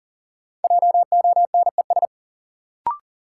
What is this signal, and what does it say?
Morse code, heard as a keyed tone, about 38 wpm, 700 Hz. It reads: JONES
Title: Sets of 5 Words - New Word Drill - 40wpm